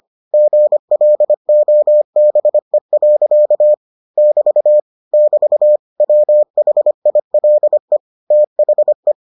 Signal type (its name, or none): Morse code